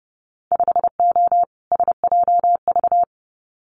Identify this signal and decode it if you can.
Morse code, heard as a keyed tone, about 30 wpm, 700 Hz. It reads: 5O SJV
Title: Winter Field Day Exchanges Encore 30wpm